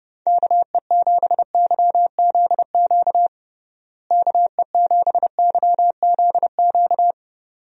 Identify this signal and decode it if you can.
Morse code, heard as a keyed tone, about 30 wpm, 700 Hz. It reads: KE7YZQ KE7YZQ
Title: USA Call Signs - 6 Characters - Repeated 2x - 30wpm